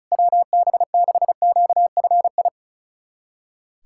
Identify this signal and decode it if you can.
Morse code, heard as a keyed tone, about 34 wpm, 700 Hz. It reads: WB6QFI
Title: USA Callsign DE Callsign - 6 Characters - 35wpm